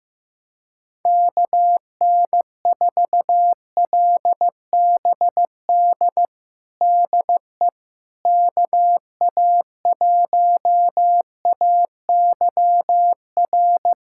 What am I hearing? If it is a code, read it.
Morse code, heard as a keyed tone, about 15 wpm, 700 Hz. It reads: KN4LBD DE KA1AYR
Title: USA Callsign DE Callsign - 6 Characters - 15wpm